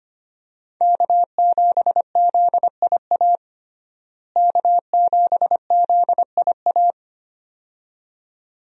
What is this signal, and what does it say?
Morse code, heard as a keyed tone, about 25 wpm, 700 Hz. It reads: K7ZIA K7ZIA
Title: USA Call Signs - 5 Characters - Repeated 2x - 25wpm